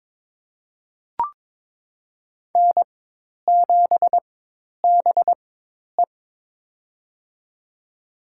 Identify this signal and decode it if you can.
Morse code, heard as a keyed tone, about 21 wpm, spaced as for 10 wpm, 700 Hz. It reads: N7BE
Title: USA Call Signs - 4 Characters - 4x Character Spacing - 22wpm